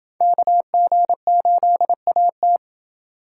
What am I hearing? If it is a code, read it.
Morse code, heard as a keyed tone, about 31 wpm, 700 Hz. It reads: KG8AT